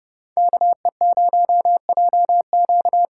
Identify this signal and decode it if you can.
Morse code, heard as a keyed tone, about 30 wpm, 700 Hz. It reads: KE0JQ
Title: USA Call Signs - 6 Characters - 8x Character Spacing - 30wpm